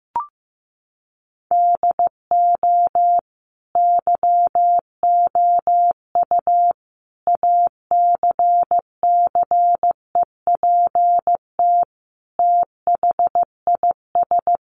Morse code, heard as a keyed tone, about 15 wpm, 700 Hz. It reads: DO YOU ACCEPT THIS
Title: Sentences from Top 1300 Words 15wpm